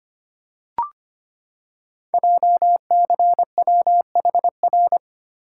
Morse code, heard as a keyed tone, about 25 wpm, 700 Hz. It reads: JCWHR
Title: Code Groups - 5 Letters - 25wpm